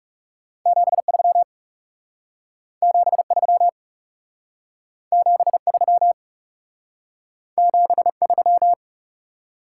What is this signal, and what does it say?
Morse code, heard as a keyed tone, about 38 wpm, 700 Hz. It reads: 73 73 73 73